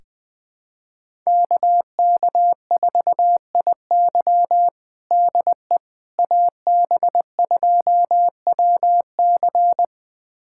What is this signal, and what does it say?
Morse code, heard as a keyed tone, about 20 wpm, 700 Hz. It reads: KK4IY DE AB2WC
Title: USA Callsign DE Callsign - 5 Characters - 20wpm